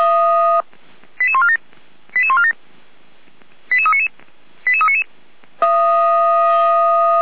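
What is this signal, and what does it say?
Five-tone selective call (ZVEI), heard as quick runs of five tones, tones of about 70 ms. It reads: 88147 88147 88399 88399
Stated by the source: ZVEI five-tone selective call received on SDR.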